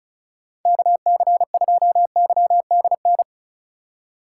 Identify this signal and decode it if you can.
Morse code, heard as a keyed tone, about 43 wpm, 700 Hz. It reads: KC2YDN
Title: Speed Racing - USA 2x3 Call Signs 35wpm